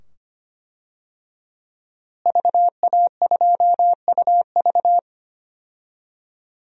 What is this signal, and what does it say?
Morse code, heard as a keyed tone, about 25 wpm, 700 Hz. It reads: VA2UV
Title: DXCC-100 Call Signs - 5 Characters - Repeated 2x - 25wpm